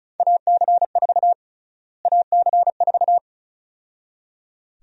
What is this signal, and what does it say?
Morse code, heard as a keyed tone, about 36 wpm, 700 Hz. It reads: AC4 AC4
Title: USA Call Sign Prefixes - Repeated 2x - 35wpm